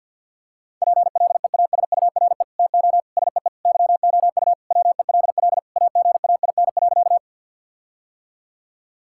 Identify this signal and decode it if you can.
Morse code, heard as a keyed tone, about 50 wpm, 700 Hz. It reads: PLEASURE TO SEE YOU WELL AGAIN.
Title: Sentences from Top 1600 Words 50wpm